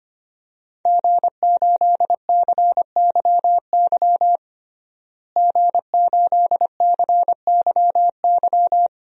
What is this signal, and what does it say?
Morse code, heard as a keyed tone, about 25 wpm, 700 Hz. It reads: G8CYY G8CYY
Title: DXCC-100 Call Signs - 5 Characters - Repeated 3x - 25wpm